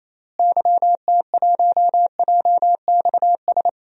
Morse code, heard as a keyed tone, about 28 wpm, 700 Hz. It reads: YT1JXS
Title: DXCC-100 Call Signs - 6 Characters - Repeated 2x - 28wpm